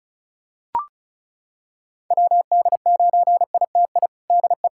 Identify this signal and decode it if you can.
Morse code, heard as a keyed tone, about 34 wpm, 700 Hz. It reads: WD9ITI DE
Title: USA Callsign DE Callsign - 6 Characters - 35wpm